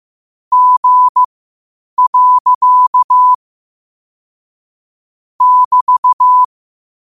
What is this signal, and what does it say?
Morse code, heard as a keyed tone, about 15 wpm, spaced as for 8 wpm, 1000 Hz. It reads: G. <BT>
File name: qotdfiles / 8wpm